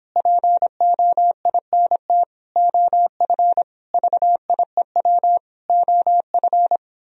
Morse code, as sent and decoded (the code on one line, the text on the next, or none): .--. --- .. -. - / --- ..-. / ...- .. .-- / --- ..-.
POINT OF VIEW OF